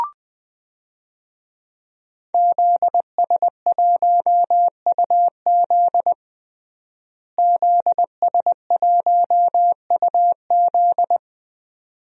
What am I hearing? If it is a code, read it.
Morse code, heard as a keyed tone, about 20 wpm, 700 Hz. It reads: ZS1UZ ZS1UZ
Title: DXCC-100 Call Signs - 5 Characters - Repeated 2x - 20wpm